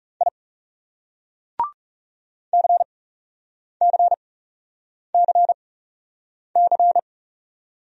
Morse code, heard as a keyed tone, about 36 wpm, spaced as for 8 wpm, 700 Hz. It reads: I CCCC
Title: Speed-Racing Single Letters 30wpm